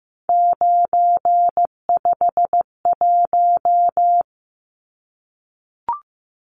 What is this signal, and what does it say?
Morse code, heard as a keyed tone, about 15 wpm, 700 Hz. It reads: 951
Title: Numbers - 3-Digit 15wpm